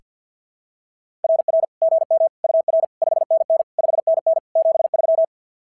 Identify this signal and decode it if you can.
Morse code, heard as a keyed tone, about 47 wpm, 660 Hz. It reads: RR GM UR 5NN 5NN 73
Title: SOTA QSO Practice 50wpm